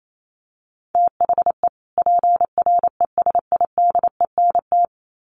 Morse code, heard as a keyed tone, about 28 wpm, 700 Hz. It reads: THE PRESIDENT
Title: Sets of 2 Words 28wpm